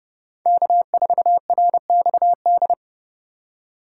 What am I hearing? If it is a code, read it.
Morse code, heard as a keyed tone, about 30 wpm, 700 Hz. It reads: K4RXD
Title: USA Call Signs - 5 Characters - 2x Character Spacing - 30wpm